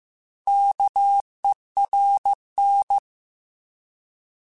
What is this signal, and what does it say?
Morse code, heard as a keyed tone, about 15 wpm, 800 Hz. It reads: KERN
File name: Alphabetical / Alphabetical mp3s / Ks @13